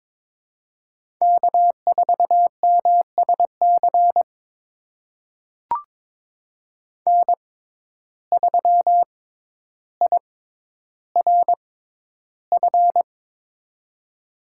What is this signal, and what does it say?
Morse code, heard as a keyed tone, about 23 wpm, spaced as for 11 wpm, 700 Hz. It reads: K4MSC N3IRF
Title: USA Call Signs - 5 Characters - 6x Character Spacing - 22wpm